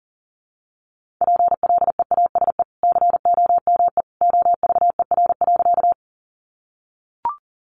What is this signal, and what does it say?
Morse code, heard as a keyed tone, about 40 wpm, 700 Hz. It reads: PLEASE COME OVER.
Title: Sentences from Top 1500 Words 40wpm